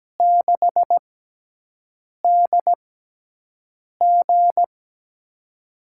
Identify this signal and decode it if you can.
Morse code, heard as a keyed tone, about 17 wpm, spaced as for 6 wpm, 700 Hz. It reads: 6DG
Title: USA Call Signs - 6x Character Spacing - 17wpm